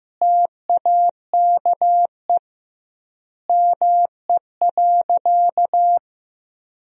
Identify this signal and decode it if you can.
Morse code, heard as a keyed tone, about 15 wpm, 700 Hz. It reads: TAKE ME.
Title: 2x Word-Spacing - Sentences from Top 400 Words - 15wpm